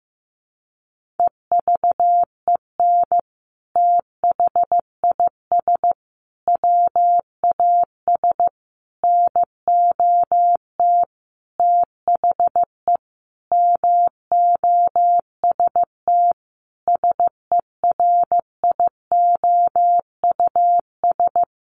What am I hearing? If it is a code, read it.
Morse code, heard as a keyed tone, about 15 wpm, 700 Hz. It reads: EVEN THIS WAS NOT THE MOST SERIOUS
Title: Sentences from Top 1100 Words 15wpm